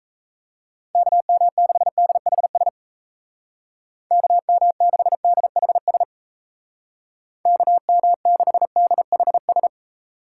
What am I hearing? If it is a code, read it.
Morse code, heard as a keyed tone, about 37 wpm, 700 Hz. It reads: KM6DHS KM6DHS KM6DHS